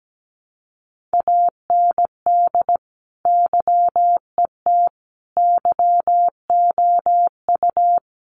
Morse code, heard as a keyed tone, about 17 wpm, 700 Hz. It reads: AND YET YOU
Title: Sentences from Top 1300 Words 17wpm